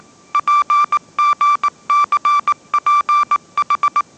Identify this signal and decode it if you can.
Morse code, heard as a keyed tone, about 19 wpm, 1170 Hz. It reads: PGCPH
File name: mrx083 20wpm